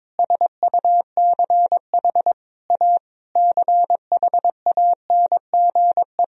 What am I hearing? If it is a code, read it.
Morse code, heard as a keyed tone, about 22 wpm, 700 Hz. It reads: SUCH A CHANGE
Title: Sentences from Top 1000 Words - Review Part 1 - 22wpm